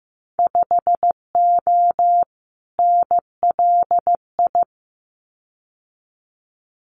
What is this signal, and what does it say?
Morse code, heard as a keyed tone, about 15 wpm, 700 Hz. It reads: 5O NLI
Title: Winter Field Day Exchanges 15wpm